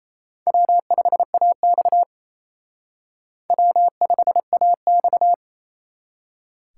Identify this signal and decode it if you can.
Morse code, heard as a keyed tone, about 34 wpm, 700 Hz. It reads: W5AX W5AX
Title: Speed Racing - USA 1x2 Call Signs 28wpm